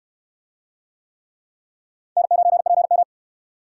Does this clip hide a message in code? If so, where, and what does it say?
Yes: Morse code from 2.2 to 3.0 s, heard as a keyed tone, about 63 wpm, 700 Hz. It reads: N9PM